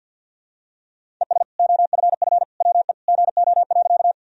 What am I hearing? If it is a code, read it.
Morse code, heard as a keyed tone, about 52 wpm, 700 Hz. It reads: ES OFF WE GO.
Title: Sentences from Top 200 Words and Abbreviations 50wpm